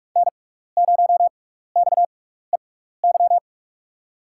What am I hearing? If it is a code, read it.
Morse code, heard as a keyed tone, about 44 wpm, spaced as for 15 wpm, 700 Hz. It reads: N0XEY
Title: USA Call Signs - 5 Characters - 6x Character Spacing - 45wpm